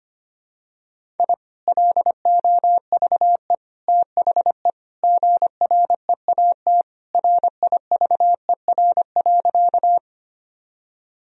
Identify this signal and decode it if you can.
Morse code, heard as a keyed tone, about 25 wpm, 700 Hz. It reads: I LOVE THE GREAT RIVER.
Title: Sentences from Top 1000 Words - Review Part 1 - 25wpm